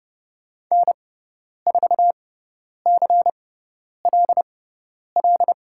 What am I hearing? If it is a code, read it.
Morse code, heard as a keyed tone, about 30 wpm, spaced as for 10 wpm, 700 Hz. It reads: N4CLL
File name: Lesson 18g - Letter m - Call Signs 30wpm